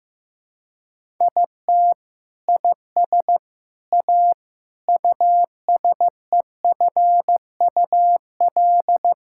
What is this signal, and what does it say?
Morse code, heard as a keyed tone, about 16 wpm, 700 Hz. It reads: IT IS A USEFUL
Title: Sentences from Top 1000 Words - Review Part 6 - 15wpm